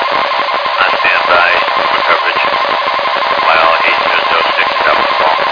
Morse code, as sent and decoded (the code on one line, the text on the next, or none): none